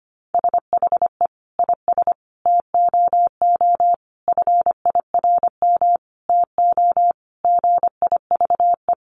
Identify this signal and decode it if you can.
Morse code, heard as a keyed tone, about 25 wpm, 700 Hz. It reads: SHE IS TOO FIRM TO GIVE
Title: Sentences from Top 1400 Words 25wpm